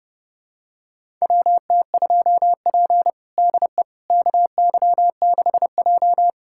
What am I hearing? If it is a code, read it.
Morse code, heard as a keyed tone, about 30 wpm, 700 Hz. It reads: WT2P DE KY6J
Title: USA Callsign DE Callsign - 4 Characters - 30wpm